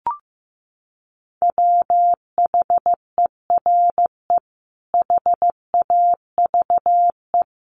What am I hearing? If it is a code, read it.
Morse code, heard as a keyed tone, about 14 wpm, 700 Hz. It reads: WHERE HAVE
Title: Sentences from Top 300 Words 15wpm